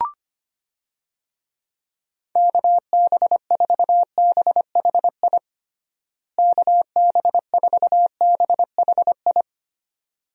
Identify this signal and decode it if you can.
Morse code, heard as a keyed tone, about 25 wpm, 700 Hz. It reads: KB4BHI KB4BHI
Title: USA Call Signs - 6 Characters - Repeated 2x - 25wpm